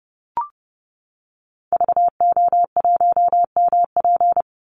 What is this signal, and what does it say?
Morse code, heard as a keyed tone, about 30 wpm, 700 Hz. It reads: VO1MP
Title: Call Signs Repeated 2x - 30wpm